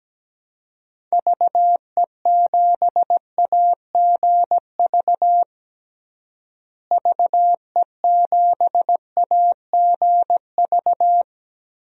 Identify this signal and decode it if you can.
Morse code, heard as a keyed tone, about 17 wpm, 700 Hz. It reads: VE7AGV VE7AGV
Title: DXCC-100 Call Signs - 6 Characters - Repeated 3x - 17wpm